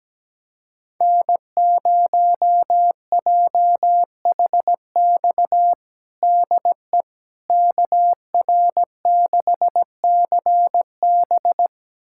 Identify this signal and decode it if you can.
Morse code, heard as a keyed tone, about 17 wpm, 700 Hz. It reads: N0JHX DE KR6CB
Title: USA Callsign DE Callsign - 5 Characters - 17wpm